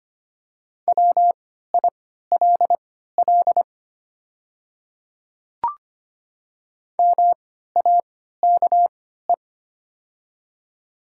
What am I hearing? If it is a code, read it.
Morse code, heard as a keyed tone, about 25 wpm, spaced as for 14 wpm, 700 Hz. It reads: WILL MAKE
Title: 3x Character Spacing - Top 100 Words - 25wpm